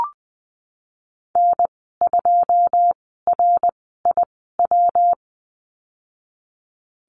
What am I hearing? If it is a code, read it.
Morse code, heard as a keyed tone, about 20 wpm, spaced as for 15 wpm, 700 Hz. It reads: N2RIW
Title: 213 Lesson 34g - Number 8 - Call Signs 20wpm